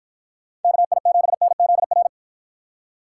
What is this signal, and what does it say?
Morse code, heard as a keyed tone, about 50 wpm, 700 Hz. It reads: KI7NZR